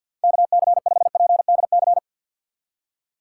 Morse code, heard as a keyed tone, about 52 wpm, 700 Hz. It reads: KC5PDC
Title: USA Call Signs - 2x Character Spacing - 50wpm